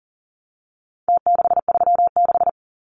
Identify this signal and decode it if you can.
Morse code, heard as a keyed tone, about 40 wpm, 700 Hz. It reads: T636
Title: Numbers - Contest Style - 4-Digit 40wpm